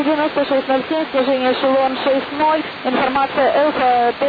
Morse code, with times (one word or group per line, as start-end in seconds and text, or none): none